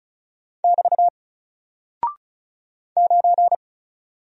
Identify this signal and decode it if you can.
Morse code, heard as a keyed tone, about 34 wpm, 700 Hz. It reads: <BT> 9
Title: Single Character (Letter:Number:Punctuation:Prosign) 35wpm